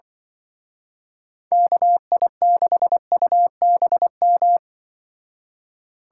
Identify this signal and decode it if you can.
Morse code, heard as a keyed tone, about 24 wpm, 700 Hz. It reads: KI6UBM
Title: Speed Racing - USA 2x3 Call Signs 20wpm